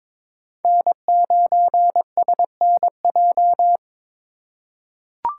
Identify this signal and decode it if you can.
Morse code, heard as a keyed tone, about 22 wpm, 700 Hz. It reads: N9SNJ
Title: USA Call Signs - 4x Character Spacing - 22wpm